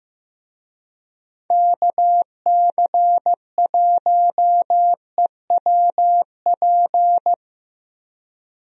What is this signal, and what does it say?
Morse code, heard as a keyed tone, about 15 wpm, 700 Hz. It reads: KC1EWP
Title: USA Call Signs - 6 Characters - Repeated 2x - 15wpm